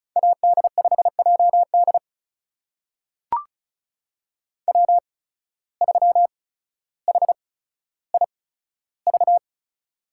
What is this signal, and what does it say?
Morse code, heard as a keyed tone, about 36 wpm, spaced as for 15 wpm, 700 Hz. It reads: AD5JD W3HIV
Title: USA Call Signs - 5 Characters - 8x Character Spacing - 35wpm